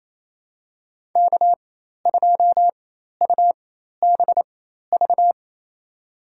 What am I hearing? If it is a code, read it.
Morse code, heard as a keyed tone, about 28 wpm, spaced as for 13 wpm, 700 Hz. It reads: K2UBV
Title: USA Call Signs - 5 Characters - 4x Character Spacing - 28wpm